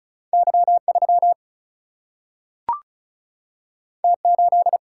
Morse code, heard as a keyed tone, about 36 wpm, 700 Hz. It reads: Y3 T8
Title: Single-Letter Single-Number 35wpm